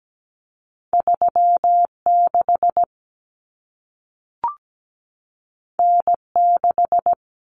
Morse code, heard as a keyed tone, about 17 wpm, 700 Hz. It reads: 36 N6
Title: Common Cut Numbers - 2-Digit 17wpm